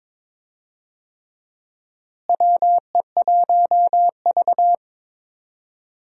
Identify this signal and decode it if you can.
Morse code, heard as a keyed tone, about 22 wpm, 700 Hz. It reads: WE1V
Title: USA Call Signs - 4 Characters - Repeated 3x - 22wpm